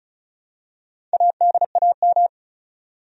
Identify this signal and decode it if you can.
Morse code, heard as a keyed tone, about 34 wpm, 700 Hz. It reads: ADAM
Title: USA Common Names - Max 4 Letters - 35wpm